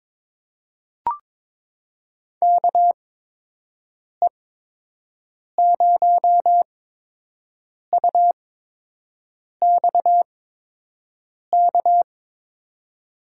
Morse code, heard as a keyed tone, about 22 wpm, spaced as for 6 wpm, 700 Hz. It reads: KE0UXK
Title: USA Call Signs - 8x Character Spacing - 22wpm